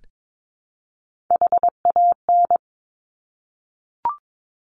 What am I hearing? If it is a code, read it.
Morse code, heard as a keyed tone, about 22 wpm, 700 Hz. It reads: HAN
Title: Top Three Letter Combinations in Words 22wpm